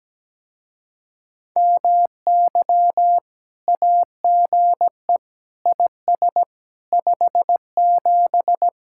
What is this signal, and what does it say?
Morse code, heard as a keyed tone, about 17 wpm, 700 Hz. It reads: MY AGE IS 57
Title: Speed-Racing Common QSO Phrases Encore - 17wpm